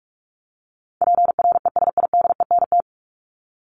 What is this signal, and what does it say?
Morse code, heard as a keyed tone, about 44 wpm, 700 Hz. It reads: PRESIDENT
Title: Top 751-1000 Words 45wpm